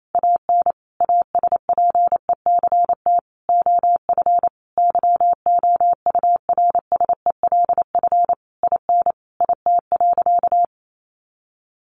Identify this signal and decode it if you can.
Morse code, heard as a keyed tone, about 28 wpm, 700 Hz. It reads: AN ASPECT OF YOURSELF IN IT.
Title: Sentences from Top 1300 Words 28wpm